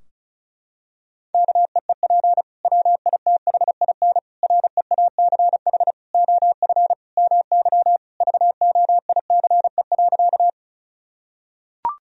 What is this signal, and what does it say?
Morse code, heard as a keyed tone, about 34 wpm, 700 Hz. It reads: KEEP WITHIN REACH OF MY VOICE.
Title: Sentences from Top 1000 Words - Review Part 4 - 35wpm